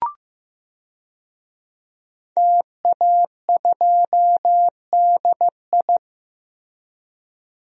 Morse code, heard as a keyed tone, about 15 wpm, 700 Hz. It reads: TA2DI
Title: DXCC-100 Call Signs - 5 Characters - Repeated 2x - 15wpm